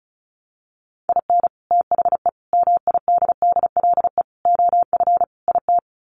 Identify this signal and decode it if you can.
Morse code, heard as a keyed tone, about 34 wpm, 700 Hz. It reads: IN THE MIDDLE OF IT
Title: Sets of 5 Words Encore 35wpm